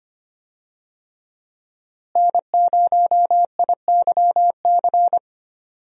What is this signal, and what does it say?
Morse code, heard as a keyed tone, about 25 wpm, 700 Hz. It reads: N0IYC
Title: USA Call Signs - 5 Characters - Repeated 3x - 25wpm